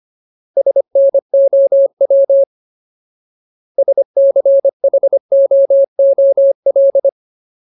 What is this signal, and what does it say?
Morse code, heard as a keyed tone, about 25 wpm, 550 Hz. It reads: SNOW SCHOOL
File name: Ultimate Word List / 25WPM Ultimate